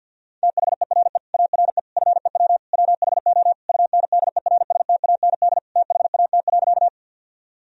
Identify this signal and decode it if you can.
Morse code, heard as a keyed tone, about 50 wpm, 700 Hz. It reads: THERE ARE FEW WHO UNDERSTAND THAT.
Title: Sentences from Top 1000 Words - Review Part 1 - 50wpm